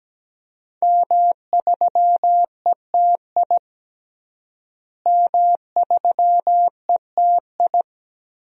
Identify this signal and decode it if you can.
Morse code, heard as a keyed tone, about 17 wpm, 700 Hz. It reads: M3ETI M3ETI
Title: DXCC-100 Call Signs - 5 Characters - Repeated 3x - 17wpm